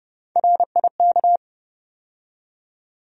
Morse code, heard as a keyed tone, about 32 wpm, 700 Hz. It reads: RIK
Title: USA Call Sign Postfixes - 3 Letters 30wpm